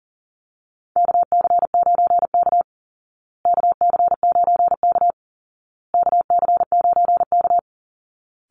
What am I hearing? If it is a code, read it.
Morse code, heard as a keyed tone, about 40 wpm, 700 Hz. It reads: KC9K KC9K KC9K
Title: Call Signs Repeated 2x - 40wpm